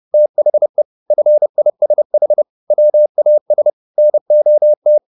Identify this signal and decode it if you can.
Morse code, heard as a keyed tone, about 30 wpm, 600 Hz. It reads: THE FISH WAS NOT
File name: the old man and the sea / the old man and the sea 1 30wpm